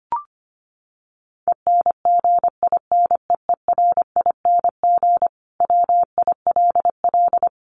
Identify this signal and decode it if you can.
Morse code, heard as a keyed tone, about 25 wpm, 700 Hz. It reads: ENGINEERING WILL
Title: Sentences from Top 1500 Words 25wpm